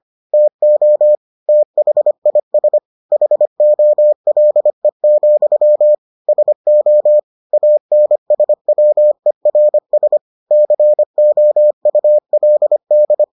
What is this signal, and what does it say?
Morse code, heard as a keyed tone, about 25 wpm, 600 Hz. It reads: TO THIS HOLE, SO ANSWERS COULD